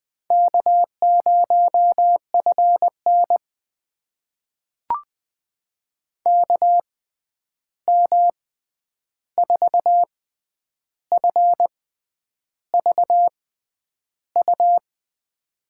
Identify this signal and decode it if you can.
Morse code, heard as a keyed tone, about 20 wpm, spaced as for 9 wpm, 700 Hz. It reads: K0FN KM4FVU
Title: USA Call Signs - 6x Character Spacing - 20wpm